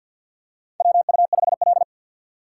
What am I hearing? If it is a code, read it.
Morse code, heard as a keyed tone, about 48 wpm, 700 Hz. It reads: WU5L
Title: USA Call Signs - 4 Characters - 6x Character Spacing - 50wpm